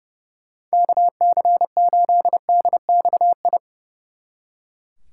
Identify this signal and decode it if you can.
Morse code, heard as a keyed tone, about 37 wpm, 700 Hz. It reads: KC8DXI
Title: Speed Racing - USA 2x3 Call Signs 30wpm